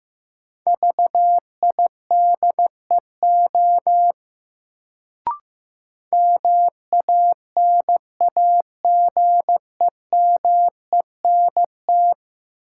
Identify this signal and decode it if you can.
Morse code, heard as a keyed tone, about 15 wpm, 700 Hz. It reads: VIDEO MANAGEMENT
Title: Sentences from Top 1000 Words - Review Part 4 - New Word Drill - 15wpm